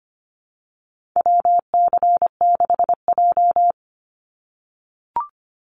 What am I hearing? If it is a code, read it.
Morse code, heard as a keyed tone, about 25 wpm, 700 Hz. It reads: WC6J
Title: NA SOTA Activator Callsigns 25wpm